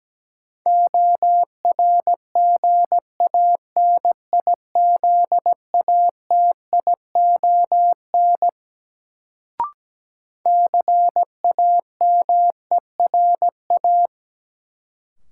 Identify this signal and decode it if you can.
Morse code, heard as a keyed tone, about 17 wpm, 700 Hz. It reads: ORGANIZATION CAMERA
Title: Sentences from Top 1000 Words - Review Part 5 - New Word Drill - 17wpm